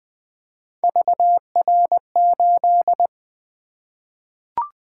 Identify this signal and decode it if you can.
Morse code, heard as a keyed tone, about 20 wpm, 700 Hz. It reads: VR8
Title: Three Letter-Number Combinations - 20wpm